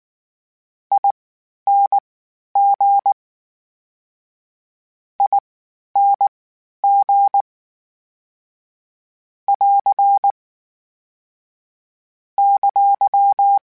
Morse code, heard as a keyed tone, about 19 wpm, spaced as for 11 wpm, 800 Hz. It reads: ING ING <AR> !
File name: General / General-mp3s / Suffixes sp @18